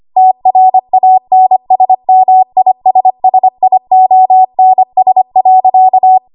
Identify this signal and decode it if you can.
Morse code, heard as a keyed tone, about 25 wpm, 750 Hz. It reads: TRANSMISSIONS.
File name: Archive / 25 WPM / 25WPM